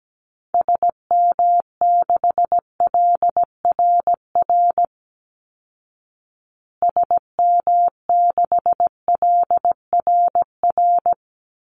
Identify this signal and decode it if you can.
Morse code, heard as a keyed tone, about 17 wpm, 700 Hz. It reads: SM6LRR SM6LRR
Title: Call Signs Repeated 2x - 17wpm